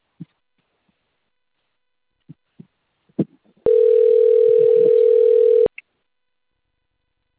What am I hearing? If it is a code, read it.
Ringback tone (North America), 440 + 480 Hz.